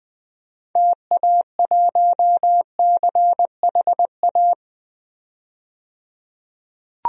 Morse code, heard as a keyed tone, about 20 wpm, 700 Hz. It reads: TA1CHA
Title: DXCC-100 Call Signs - 6 Characters - 20wpm